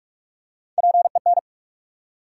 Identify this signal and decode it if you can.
Morse code, heard as a keyed tone, about 42 wpm, 700 Hz. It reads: PEN
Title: Three Letter Words - 45wpm